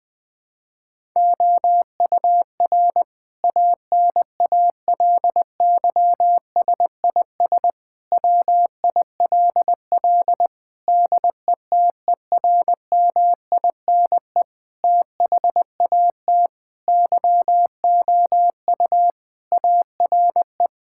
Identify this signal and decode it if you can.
Morse code, heard as a keyed tone, about 20 wpm, 700 Hz. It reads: OUR ANALYSIS WILL DETERMINE THAT YOU ARE
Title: Sentences from Top 1000 Words - Review Part 5 - 20wpm